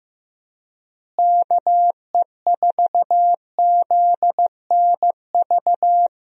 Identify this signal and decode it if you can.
Morse code, heard as a keyed tone, about 15 wpm, 700 Hz. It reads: KE4ZNV
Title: USA Call Signs - 6 Characters - Repeated 3x - 15wpm